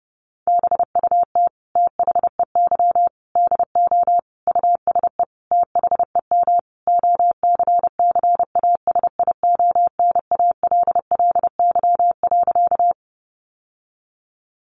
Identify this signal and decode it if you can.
Morse code, heard as a keyed tone, about 30 wpm, 700 Hz. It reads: BUT THEY DO USE THEM OCCASIONALLY.
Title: Sentences from Top 1300 Words 30wpm